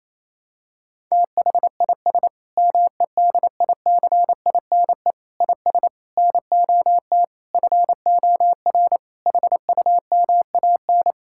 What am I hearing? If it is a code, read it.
Morse code, heard as a keyed tone, about 28 wpm, 700 Hz. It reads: THIS MEDICINE IS NOT FOR HUMAN
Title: Sentences from Top 1000 Words - Review Part 6 - 28wpm